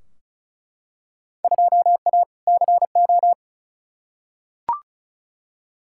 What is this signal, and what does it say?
Morse code, heard as a keyed tone, about 34 wpm, 700 Hz. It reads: 2A CO
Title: Field Day Exchanges - 35wpm